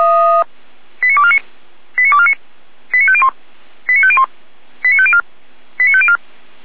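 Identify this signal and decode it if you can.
Five-tone selective call (ZVEI), heard as quick runs of five tones, tones of about 70 ms. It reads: 89249 89249 89501 89501 89504 89504
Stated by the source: ZVEI five-tone selective call 89163 received by SDR.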